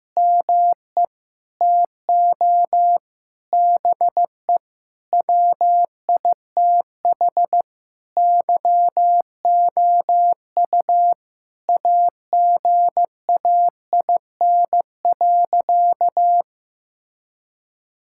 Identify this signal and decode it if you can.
Morse code, heard as a keyed tone, about 16 wpm, 700 Hz. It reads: ME TO BE WITH YOU AGAIN.
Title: Sentences from Top 1600 Words 15wpm